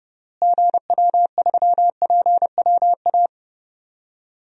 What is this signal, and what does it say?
Morse code, heard as a keyed tone, about 31 wpm, 700 Hz. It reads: GW3PWA
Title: DXCC-100 Call Signs - 6 Characters - Repeated 3x - 30wpm